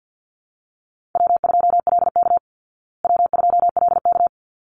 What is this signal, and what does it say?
Morse code, heard as a keyed tone, about 48 wpm, 700 Hz. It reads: W2LK W2LK
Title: Call Signs Repeated 3x - 50wpm